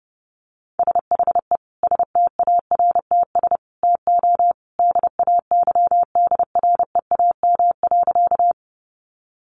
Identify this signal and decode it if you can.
Morse code, heard as a keyed tone, about 30 wpm, 700 Hz. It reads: SHE STARTS TO DAYDREAM.
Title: Easy Phrases 30wpm